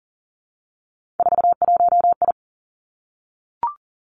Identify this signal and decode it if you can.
Morse code, heard as a keyed tone, about 38 wpm, 690 Hz. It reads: 41I
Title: Three Challenging Letter-Number Combinations 40wpm